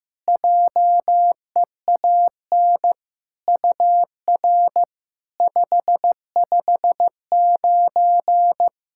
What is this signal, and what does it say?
Morse code, heard as a keyed tone, about 15 wpm, 690 Hz. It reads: JEAN UR 559
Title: SOTA QSO Practice 15wpm